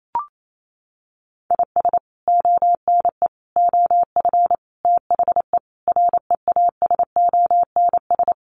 Morse code, heard as a keyed tone, about 28 wpm, 700 Hz. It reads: IS ONE OF THE REASONS
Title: Sets of 5 Words Encore2 28wpm